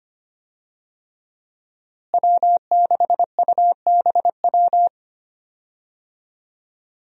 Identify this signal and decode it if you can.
Morse code, heard as a keyed tone, about 25 wpm, 700 Hz. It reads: W6UBW
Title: USA Call Signs - 5 Characters - Repeated 3x - 25wpm